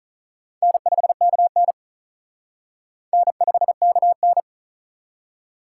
Morse code, heard as a keyed tone, about 46 wpm, 700 Hz. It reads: N5KN N5KN